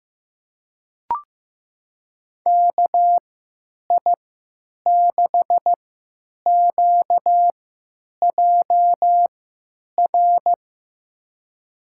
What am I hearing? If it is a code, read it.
Morse code, heard as a keyed tone, about 15 wpm, spaced as for 8 wpm, 700 Hz. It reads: KI6QJR
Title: USA Call Signs - 3x Character Spacing - 15wpm